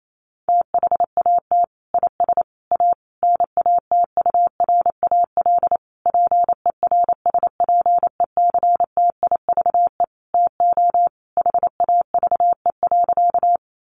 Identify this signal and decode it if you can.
Morse code, heard as a keyed tone, about 28 wpm, 700 Hz. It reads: THAT IS A NATURAL PERSPECTIVE TO HAVE.
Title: Sentences from Top 1200 Words 28wpm